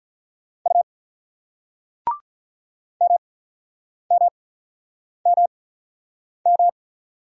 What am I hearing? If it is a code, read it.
Morse code, heard as a keyed tone, about 48 wpm, spaced as for 9 wpm, 700 Hz. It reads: U MMMM